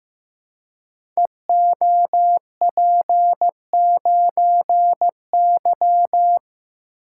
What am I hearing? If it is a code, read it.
Morse code, heard as a keyed tone, about 15 wpm, 700 Hz. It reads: EOP9Y
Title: Code Groups - 5 Letter:Numbers - 15wpm